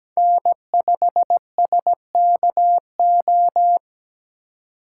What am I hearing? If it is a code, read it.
Morse code, heard as a keyed tone, about 20 wpm, 700 Hz. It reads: N5SKO
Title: Speed Racing - USA 1x3 Call Signs 17wpm